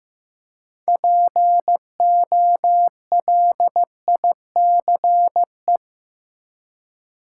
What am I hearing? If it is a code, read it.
Morse code, heard as a keyed tone, about 15 wpm, 700 Hz. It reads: POLICE
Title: Six Letter Words - 15wpm